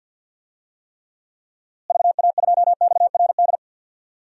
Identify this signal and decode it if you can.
Morse code, heard as a keyed tone, about 49 wpm, 700 Hz. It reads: VA2XRD
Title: DXCC-100 Call Signs - 6 Characters - Repeated 3x - 50wpm